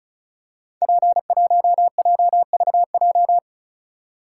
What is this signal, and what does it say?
Morse code, heard as a keyed tone, about 36 wpm, 700 Hz. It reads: P1JVJ
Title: Code Groups - 5 Letter:Numbers - 35wpm